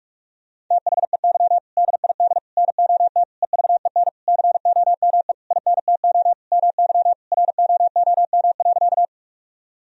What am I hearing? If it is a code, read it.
Morse code, heard as a keyed tone, about 41 wpm, 700 Hz. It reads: THEY DID NOT EVEN COME INTO MY ROOM.
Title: Sentences from Top 1000 Words - Review Part 1 - 45wpm